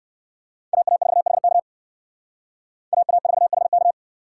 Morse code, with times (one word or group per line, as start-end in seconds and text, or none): 0.7-1.6 AA4HK
2.9-3.9 AA4HK